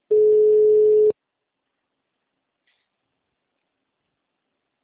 Ringback tone (Europe, 425 Hz).